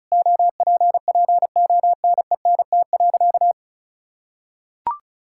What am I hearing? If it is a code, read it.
Morse code, heard as a keyed tone, about 34 wpm, 700 Hz. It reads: OPPONENT.
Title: Pangram Challenge v2 - 35wpm